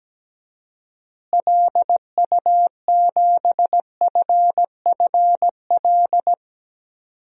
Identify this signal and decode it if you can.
Morse code, heard as a keyed tone, about 17 wpm, 700 Hz. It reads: LU7FFL
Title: DXCC-100 Call Signs - 6 Characters - Repeated 3x - 17wpm